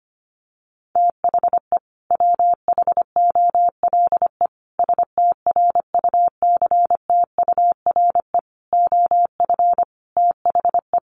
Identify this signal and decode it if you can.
Morse code, heard as a keyed tone, about 25 wpm, 700 Hz. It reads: THE WHOLE STRUCTURE OF THE
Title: Sentences from Top 1500 Words 25wpm